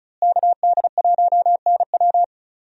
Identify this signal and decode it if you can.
Morse code, heard as a keyed tone, about 36 wpm, 700 Hz. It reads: KD1NW
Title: USA Call Signs - 5 Characters - Repeated 3x - 35wpm